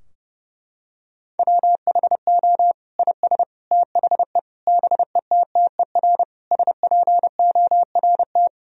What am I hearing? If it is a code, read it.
Morse code, heard as a keyed tone, about 32 wpm, 700 Hz. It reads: WHO IS THE BETTER SPORT
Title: Sentences from Top 1000 Words - Review Part 4 - 30wpm